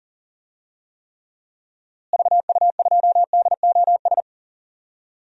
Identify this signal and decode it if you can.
Morse code, heard as a keyed tone, about 40 wpm, 700 Hz. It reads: VU2DOS
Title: DXCC-100 Call Signs - 6 Characters - Repeated 2x - 40wpm